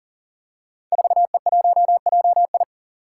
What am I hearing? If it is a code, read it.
Morse code, heard as a keyed tone, about 39 wpm, 700 Hz. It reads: 4E1JI
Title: Five Challenging Letter-Number Combinations 40wpm